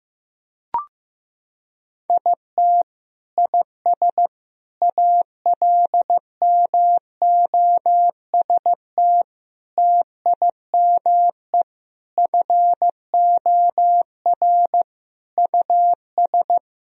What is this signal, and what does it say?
Morse code, heard as a keyed tone, about 15 wpm, 700 Hz. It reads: IT IS ALMOST TIME FOR US
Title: Sentences from Top 1000 Words - Review Part 4 - 15wpm